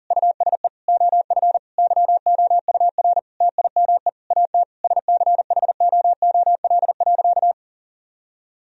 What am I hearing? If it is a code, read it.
Morse code, heard as a keyed tone, about 40 wpm, 700 Hz. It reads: USE OF YOUR TIME AT SCHOOL.
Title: Sentences from Top 1000 Words - Review Part 1 - 40wpm